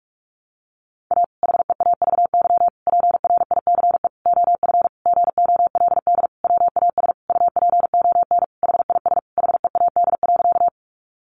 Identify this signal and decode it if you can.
Morse code, heard as a keyed tone, about 44 wpm, 700 Hz. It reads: A HEAVY PRICE OF GOLD WAS UPON HIS HEAD.
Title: Sentences from Top 1100 Words 45wpm